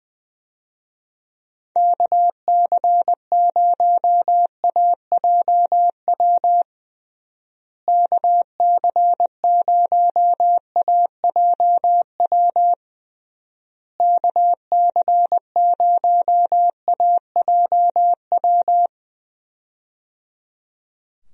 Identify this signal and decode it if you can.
Morse code, heard as a keyed tone, about 20 wpm, 700 Hz. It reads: KC0AJW KC0AJW KC0AJW
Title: USA Call Signs - 6 Characters - Repeated 3x - 20wpm